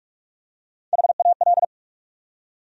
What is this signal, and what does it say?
Morse code, heard as a keyed tone, about 46 wpm, 700 Hz. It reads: HAL
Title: USA Common Names - Max 4 Letters - 45wpm